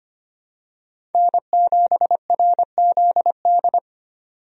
Morse code, heard as a keyed tone, about 25 wpm, 700 Hz. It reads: N7RZD
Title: USA Call Signs - 5 Characters - 25wpm